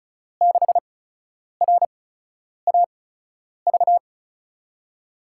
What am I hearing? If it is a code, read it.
Morse code, heard as a keyed tone, about 34 wpm, spaced as for 10 wpm, 700 Hz. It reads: 6RAV
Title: USA Call Signs - 6 Characters - 8x Character Spacing - 35wpm